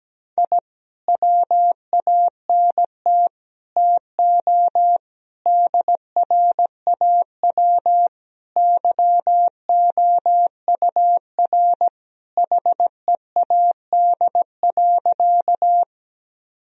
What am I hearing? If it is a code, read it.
Morse code, heard as a keyed tone, about 17 wpm, 700 Hz. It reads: I WANT TO DRAW YOUR HEAD.
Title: Sentences from Top 1000 Words - Review Part 1 - 17wpm